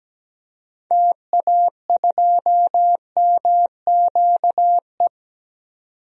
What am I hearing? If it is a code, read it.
Morse code, heard as a keyed tone, about 17 wpm, 700 Hz. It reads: TA2MQE
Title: DXCC-100 Call Signs - 6 Characters - 17wpm